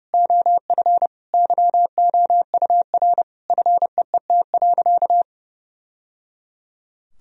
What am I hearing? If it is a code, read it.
Morse code, heard as a keyed tone, about 30 wpm, 700 Hz. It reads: OF YOUR FEET.
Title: Sentences from Top 1000 Words - Review Part 1 - 30wpm